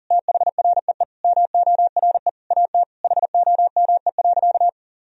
Morse code, heard as a keyed tone, about 40 wpm, 700 Hz. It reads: THREE MORE AT HOME.
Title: Sentences from Top 1000 Words - Review Part 1 - 40wpm